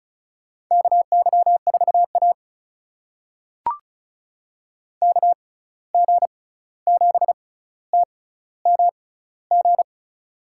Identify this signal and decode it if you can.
Morse code, heard as a keyed tone, about 36 wpm, spaced as for 16 wpm, 700 Hz. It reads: KY4A KG7TMG
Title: USA Call Signs - 6x Character Spacing - 35wpm